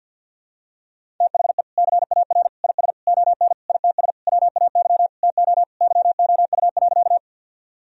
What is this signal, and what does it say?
Morse code, heard as a keyed tone, about 51 wpm, 700 Hz. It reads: THE CAR IS ON ITS WAY TO YOU.
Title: Sentences from Top 1000 Words - Review Part 1 - 50wpm